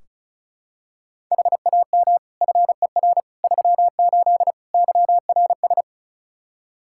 Morse code, heard as a keyed tone, about 36 wpm, 700 Hz. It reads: HAM FER 38 YRS
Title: K1USN SST QSO Practice 35wpm